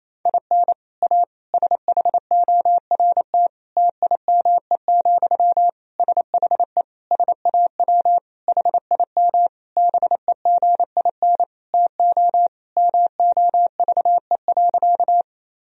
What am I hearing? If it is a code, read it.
Morse code, heard as a keyed tone, about 28 wpm, 700 Hz. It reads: IN A SHORT TIME, SHE SAW HIM BEGIN TO MOVE.
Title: Sentences from Top 1000 Words - Review Part 2 - 28wpm